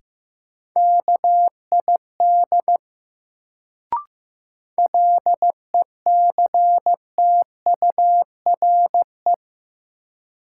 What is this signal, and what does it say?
Morse code, heard as a keyed tone, about 16 wpm, 700 Hz. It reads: KID LECTURE
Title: Sentences from Top 1600 Words - New Word Drill Alt - 15wpm